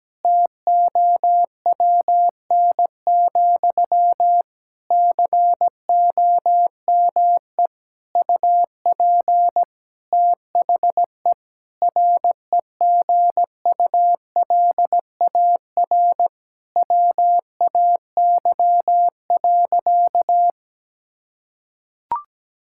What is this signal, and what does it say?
Morse code, heard as a keyed tone, about 17 wpm, 700 Hz. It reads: TOWN, COME UP THE REGULAR WAY.
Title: Sentences from Top 1000 Words - Review Part 6 - 17wpm